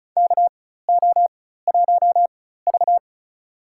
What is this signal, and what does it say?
Morse code, heard as a keyed tone, about 34 wpm, spaced as for 16 wpm, 700 Hz. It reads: KO1V
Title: USA Call Signs - 4 Characters - 4x Character Spacing - 35wpm